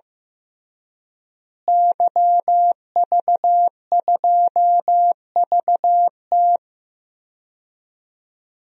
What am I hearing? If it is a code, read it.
Morse code, heard as a keyed tone, about 15 wpm, 700 Hz. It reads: YV2VT
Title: DXCC-100 Call Signs - 5 Characters - Repeated 3x - 15wpm